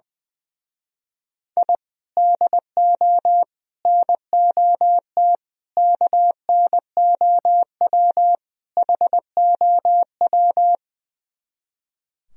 Morse code, heard as a keyed tone, about 21 wpm, 700 Hz. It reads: I DO NOT KNOW HOW
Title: Speed-Racing - Sets of 5 Words 20wpm